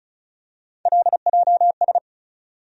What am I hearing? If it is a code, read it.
Morse code, heard as a keyed tone, about 36 wpm, 700 Hz. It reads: LJS
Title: USA Call Sign Postfixes - 3 Letters 35wpm